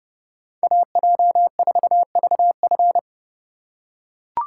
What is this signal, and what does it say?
Morse code, heard as a keyed tone, about 29 wpm, 700 Hz. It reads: AJ4VF